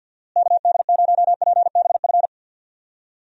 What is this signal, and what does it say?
Morse code, heard as a keyed tone, about 51 wpm, 700 Hz. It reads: KD0PBF
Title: USA Call Signs - 6 Characters - Repeated 3x - 50wpm